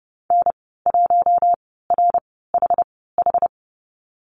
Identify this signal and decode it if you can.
Morse code, heard as a keyed tone, about 30 wpm, spaced as for 17 wpm, 700 Hz. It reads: N1RHH
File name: Lesson 14g - Number 2 - Call Signs 30wpm